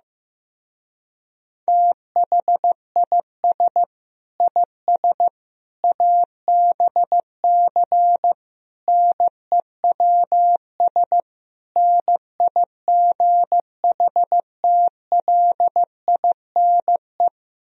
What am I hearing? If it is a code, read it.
Morse code, heard as a keyed tone, about 15 wpm, 700 Hz. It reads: THIS IS ABC NEWS NIGHTLINE
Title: Speed-Racing - Sets of 5 Words 15wpm